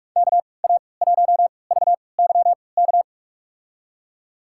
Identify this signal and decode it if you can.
Morse code, heard as a keyed tone, about 44 wpm, spaced as for 25 wpm, 700 Hz. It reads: KA1VYK
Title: USA Call Signs - 3x Character Spacing - 45wpm